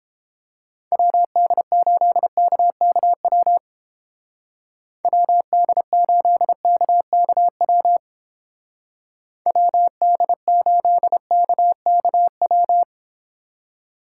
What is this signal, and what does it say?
Morse code, heard as a keyed tone, about 29 wpm, 700 Hz. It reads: WD8KKW WD8KKW WD8KKW